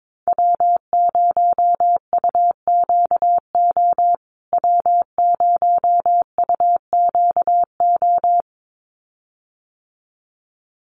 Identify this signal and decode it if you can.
Morse code, heard as a keyed tone, about 22 wpm, 700 Hz. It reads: W0UQO W0UQO
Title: USA Call Signs Repeated 2x 22wpm